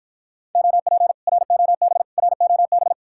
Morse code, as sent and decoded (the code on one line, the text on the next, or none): --- .--. / .-. --- -... / .-. --- -...
OP ROB ROB